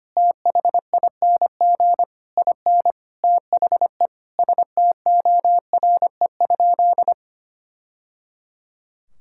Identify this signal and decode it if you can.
Morse code, heard as a keyed tone, about 25 wpm, 700 Hz. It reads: THING IN THE STORE?
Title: Sentences from Top 1000 Words - Review Part 6 - 25wpm